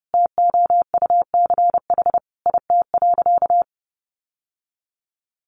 Morse code, heard as a keyed tone, about 30 wpm, 700 Hz. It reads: TOUCH IT.
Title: Sentences from Top 1200 Words 30wpm